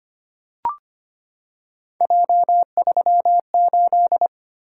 Morse code, heard as a keyed tone, about 25 wpm, 700 Hz. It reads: J38
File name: Three Letter-Number Combinations - 25wpm